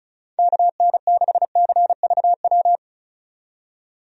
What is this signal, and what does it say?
Morse code, heard as a keyed tone, about 36 wpm, 700 Hz. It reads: KN6CVW
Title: USA Call Signs - 6 Characters - 35wpm